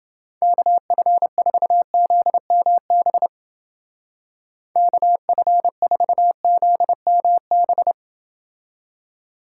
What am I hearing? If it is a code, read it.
Morse code, heard as a keyed tone, about 29 wpm, 700 Hz. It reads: KF4ZMB KF4ZMB